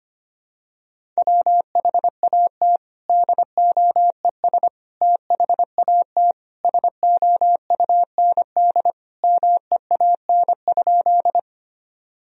Morse code, heard as a keyed tone, about 25 wpm, 700 Hz. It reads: WHAT DOES THAT SOUND MEAN?
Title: Sentences from Top 1000 Words - Review Part 1 - 25wpm